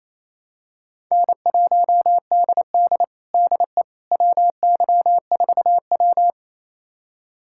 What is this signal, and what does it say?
Morse code, heard as a keyed tone, about 28 wpm, 700 Hz. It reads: N1DD DE WY4W
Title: USA Callsign DE Callsign - 4 Characters - 28wpm